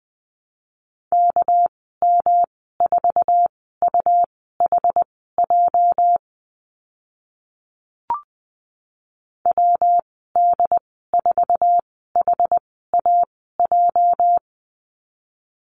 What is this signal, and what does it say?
Morse code, heard as a keyed tone, about 20 wpm, spaced as for 15 wpm, 700 Hz. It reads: KM4UHJ WD4HAJ
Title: 213 Lesson 34g - Number 8 - Call Signs 20wpm